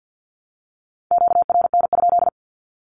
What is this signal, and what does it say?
Morse code, heard as a keyed tone, about 49 wpm, 700 Hz. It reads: QRN?
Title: Q-Codes Repeated 3x 50wpm